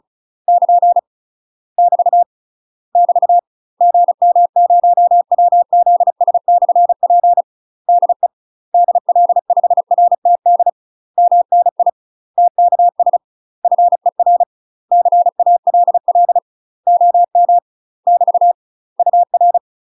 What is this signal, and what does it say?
Morse code, heard as a keyed tone, about 36 wpm, 700 Hz. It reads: <KN> <BT> <BT> GM0WZS/P DE DL5RTD MNI TKS FER CALL OM <BT> UR
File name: UK MASTER QSOs TRAINING / 35WPM / UK MASTER QSOs 0041